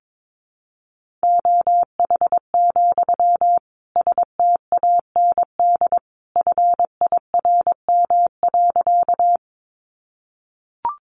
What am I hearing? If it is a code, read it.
Morse code, heard as a keyed tone, about 22 wpm, 700 Hz. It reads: OH, STAND FIRM.
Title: Sentences from Top 1400 Words 22wpm